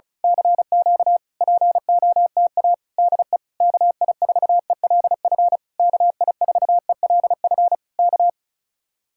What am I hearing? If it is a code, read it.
Morse code, heard as a keyed tone, about 33 wpm, 700 Hz. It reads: CQ POTA DE KI4ELF KI4ELF K
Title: Speed-Racing Common QSO Phrases Encore - 35wpm